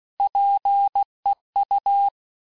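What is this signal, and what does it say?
Morse code, heard as a keyed tone, about 16 wpm, 800 Hz. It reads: PEU
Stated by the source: Morse course text, end of QSO, sent at 15 words per minute.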